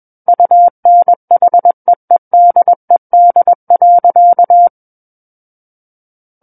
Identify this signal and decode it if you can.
Morse code, heard as a keyed tone, about 21 wpm, 700 Hz. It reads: UNHEEDED.
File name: Book One / 12-Chapter 12 - 21 WPM